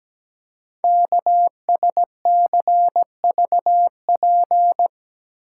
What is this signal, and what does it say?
Morse code, heard as a keyed tone, about 17 wpm, 700 Hz. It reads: KSCVP
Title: Code Groups - 5 Letters - 17wpm